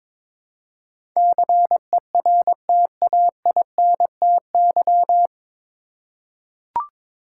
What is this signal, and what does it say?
Morse code, heard as a keyed tone, about 22 wpm, 700 Hz. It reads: CERTAINTY
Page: Morse Code Ninja practice 22wpm